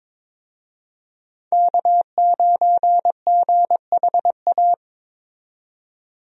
Morse code, heard as a keyed tone, about 22 wpm, 700 Hz. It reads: K9GHA
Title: USA Call Signs - 5 Characters - 22wpm